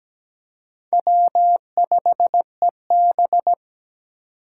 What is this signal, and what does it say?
Morse code, heard as a keyed tone, about 20 wpm, 700 Hz. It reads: W5EB
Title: Speed Racing - USA Call Signs 17wpm